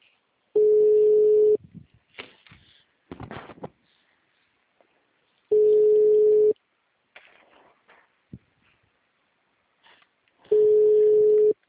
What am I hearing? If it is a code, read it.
Ringback tone (Europe, 425 Hz).
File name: out- -2015- - - .49